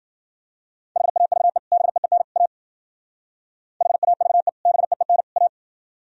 Morse code, heard as a keyed tone, about 48 wpm, 700 Hz. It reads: HAVE BEEN A HAVE BEEN A
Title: Speed-Racing - Sets of 3 Words 40wpm